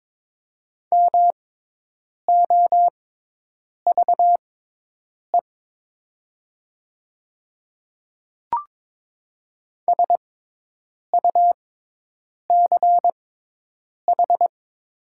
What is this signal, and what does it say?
Morse code, heard as a keyed tone, about 22 wpm, spaced as for 8 wpm, 700 Hz. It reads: MOVE SUCH
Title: 6x Character Spacing - Top 300 Words - 22wpm